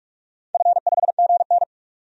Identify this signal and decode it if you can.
Morse code, heard as a keyed tone, about 46 wpm, 700 Hz. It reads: F5GN
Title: DXCC-100 Call Signs - 4 Characters - 45wpm